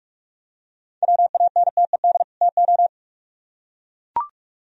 Morse code, heard as a keyed tone, about 44 wpm, 700 Hz. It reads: WANTED TO